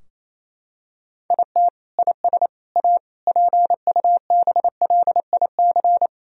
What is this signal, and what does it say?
Morse code, heard as a keyed tone, about 28 wpm, 700 Hz. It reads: IT IS A PUBLIC
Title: Sentences from Top 1000 Words - Review Part 3 - 28wpm